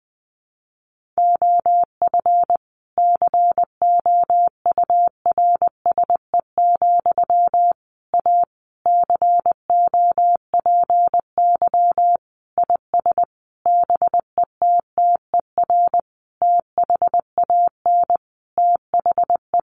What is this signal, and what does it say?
Morse code, heard as a keyed tone, about 20 wpm, 700 Hz. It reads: OF COURSE, A COPY IS BETTER THAN THE
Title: Sentences from Top 1400 Words 20wpm